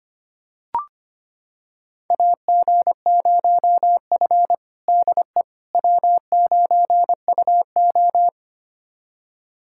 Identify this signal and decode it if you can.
Morse code, heard as a keyed tone, about 25 wpm, 700 Hz. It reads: AG0F DE W9UO
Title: USA Callsign DE Callsign - 4 Characters - 25wpm